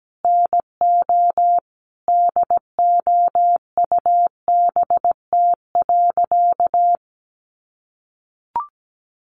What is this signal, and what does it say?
Morse code, heard as a keyed tone, about 17 wpm, 700 Hz. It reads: NO DOUBT.
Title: Sentences from Top 1500 Words 17wpm